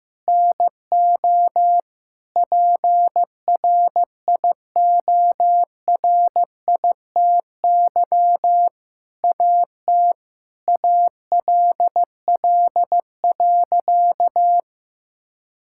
Morse code, heard as a keyed tone, about 15 wpm, 700 Hz. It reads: NO PRIORITY AT ALL.
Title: Sentences from Top 1600 Words 15wpm